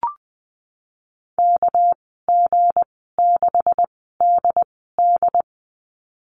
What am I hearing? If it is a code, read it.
Morse code, heard as a keyed tone, about 20 wpm, spaced as for 15 wpm, 700 Hz. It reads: KG6DD
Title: 207 Lesson 33g - Letter j - Call Signs 20wpm